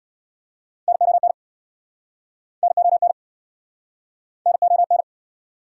Morse code, heard as a keyed tone, about 69 wpm, 700 Hz. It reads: NON NON NON